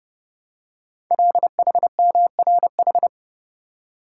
Morse code, heard as a keyed tone, about 30 wpm, 700 Hz. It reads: LHMRH
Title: Code Groups - 5 Letters - 30wpm